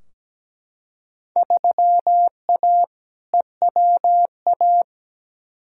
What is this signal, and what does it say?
Morse code, heard as a keyed tone, about 17 wpm, 700 Hz. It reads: 3A EWA
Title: Field Day Exchanges Encore - 17wpm